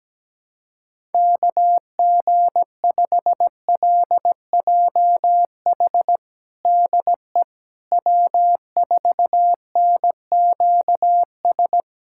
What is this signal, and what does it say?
Morse code, heard as a keyed tone, about 17 wpm, 700 Hz. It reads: KG5LJH DE W4NQS
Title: USA Callsign DE Callsign 17wpm